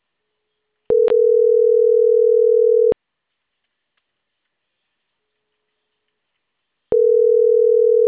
Ringback tone (North America), 440 + 480 Hz.